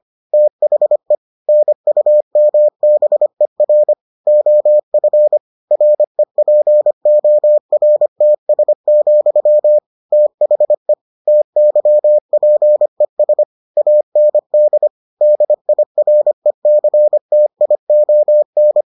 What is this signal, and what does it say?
Morse code, heard as a keyed tone, about 25 wpm, 600 Hz. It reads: THE NUMBER OF REPORTS, THE TYPES AND DIRECTION